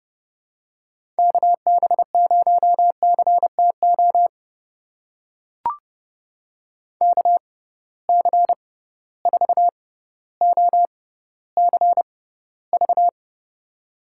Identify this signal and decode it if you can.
Morse code, heard as a keyed tone, about 30 wpm, spaced as for 15 wpm, 700 Hz. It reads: KB0CTO KC4OCV
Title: USA Call Signs - 6 Characters - 6x Character Spacing - 30wpm